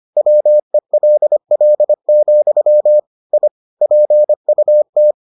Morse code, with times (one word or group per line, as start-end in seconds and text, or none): 0.2-3.0 WELL,
3.3-3.5 I
3.8-5.1 PUT